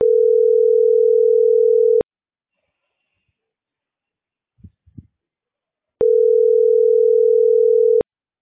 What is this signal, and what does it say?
Ringback tone (North America), 440 + 480 Hz.